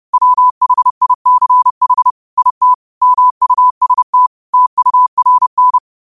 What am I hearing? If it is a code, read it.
Morse code, heard as a keyed tone, about 30 wpm, 1000 Hz. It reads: WHICH IT MUST TURN
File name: qotdfiles / 30wpm